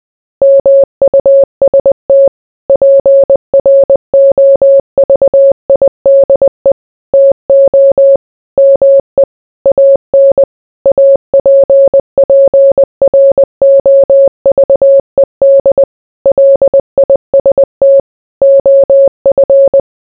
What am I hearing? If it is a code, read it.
Morse code, heard as a keyed tone, about 20 wpm, 550 Hz. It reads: MUST PROVIDE TO ME AN APPROVED LIST OF